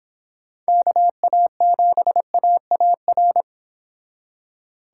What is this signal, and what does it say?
Morse code, heard as a keyed tone, about 26 wpm, 700 Hz. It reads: KA7AAR